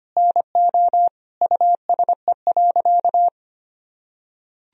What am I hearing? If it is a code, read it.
Morse code, heard as a keyed tone, about 25 wpm, 700 Hz. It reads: NO USE.
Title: Sentences from Top 1000 Words - Review Part 4 - 25wpm